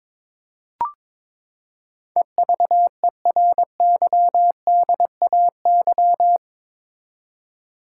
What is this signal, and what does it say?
Morse code, heard as a keyed tone, about 22 wpm, 700 Hz. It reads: EVERYDAY
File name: practice / 22wpm / Eight Letter Words - 22wpm